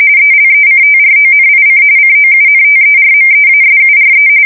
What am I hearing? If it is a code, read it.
RTTY, about 45.45 baud. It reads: ELECTRONIC COUNTERMEASURE
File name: fsk170 / USS-New-Jersey